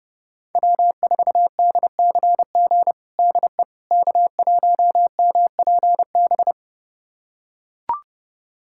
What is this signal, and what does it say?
Morse code, heard as a keyed tone, about 32 wpm, 700 Hz. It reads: W4DCG DE K1MPB
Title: USA Callsign DE Callsign - 5 Characters - 30wpm